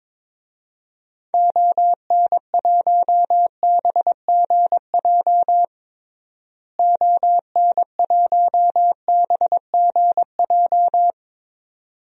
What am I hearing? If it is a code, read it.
Morse code, heard as a keyed tone, about 22 wpm, 700 Hz. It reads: ON1BGJ ON1BGJ
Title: DXCC-100 Call Signs - 6 Characters - Repeated 2x - 22wpm